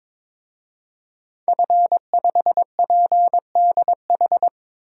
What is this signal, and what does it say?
Morse code, heard as a keyed tone, about 21 wpm, 700 Hz. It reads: F5PDH
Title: DXCC-100 Call Signs - 5 Characters - 22wpm